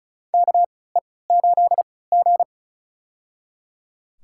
Morse code, heard as a keyed tone, about 36 wpm, spaced as for 20 wpm, 700 Hz. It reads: KE8G
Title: USA Call Signs - 4 Characters - 3x Character Spacing - 35wpm